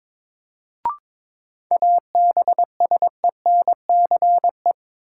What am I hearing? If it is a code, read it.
Morse code, heard as a keyed tone, about 21 wpm, 700 Hz. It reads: ABSENCE
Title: Sets of 4 Words - New Word Drill - 22wpm